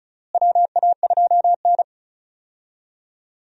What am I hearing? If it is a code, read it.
Morse code, heard as a keyed tone, about 36 wpm, 700 Hz. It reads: WA2N
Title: USA Call Signs - 4 Characters - Repeated 2x - 35wpm